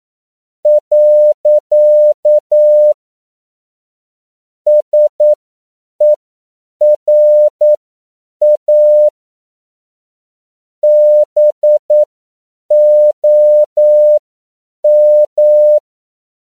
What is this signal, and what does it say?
Morse code, heard as a keyed tone, about 9 wpm, spaced as for 7 wpm, 600 Hz. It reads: . SERA BOM